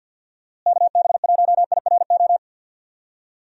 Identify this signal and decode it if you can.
Morse code, heard as a keyed tone, about 51 wpm, 700 Hz. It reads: KB1IRO
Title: USA Call Signs - 6 Characters - Repeated 3x - 50wpm